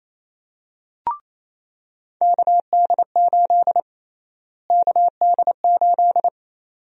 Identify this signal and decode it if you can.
Morse code, heard as a keyed tone, about 28 wpm, 700 Hz. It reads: KD8 KD8
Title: USA Call Sign Prefixes - Repeated 2x - 28wpm